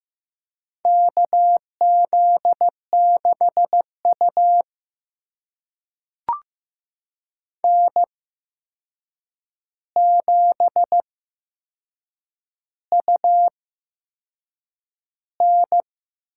Morse code, heard as a keyed tone, about 15 wpm, spaced as for 6 wpm, 700 Hz. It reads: KZ6U N7UN
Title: USA Call Signs - 4 Characters - 8x Character Spacing - 15wpm